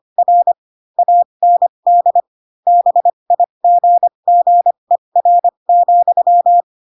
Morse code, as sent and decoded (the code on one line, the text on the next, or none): .-. / .- -. -.. / -... .. --. --. .-. --..--
R AND BIGGER,